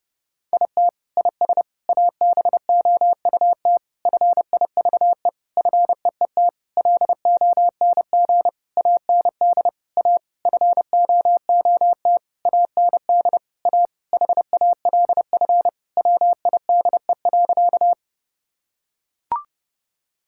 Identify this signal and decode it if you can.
Morse code, heard as a keyed tone, about 30 wpm, 700 Hz. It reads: IT IS ABOUT FIVE FEET LONG AND A FOOT AND A HALF WIDE.
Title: Sentences from Top 1000 Words - Review Part 6 - 30wpm